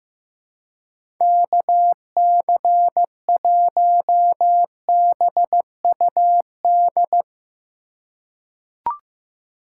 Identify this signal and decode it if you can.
Morse code, heard as a keyed tone, about 15 wpm, 700 Hz. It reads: KC1BUD
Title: USA Call Signs - 6 Characters - 3x Character Spacing - 15wpm